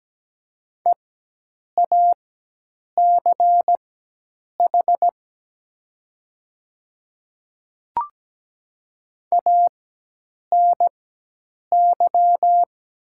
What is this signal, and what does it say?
Morse code, heard as a keyed tone, about 17 wpm, spaced as for 8 wpm, 700 Hz. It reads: EACH ANY
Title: 4x Character Spacing - Top 200 Words - 17wpm